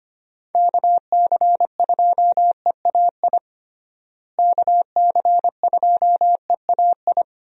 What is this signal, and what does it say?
Morse code, heard as a keyed tone, about 25 wpm, 700 Hz. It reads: KC2EAI KC2EAI
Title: USA Call Signs - 6 Characters - Repeated 2x - 25wpm